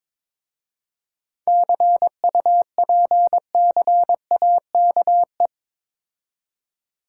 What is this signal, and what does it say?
Morse code, heard as a keyed tone, about 23 wpm, 700 Hz. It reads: CUPCAKE
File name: Difficult Words with Hints - 22wpm